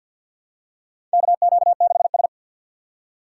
Morse code, heard as a keyed tone, about 54 wpm, 700 Hz. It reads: KQ6S
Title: USA Call Signs - 4 Characters - 6x Character Spacing - 50wpm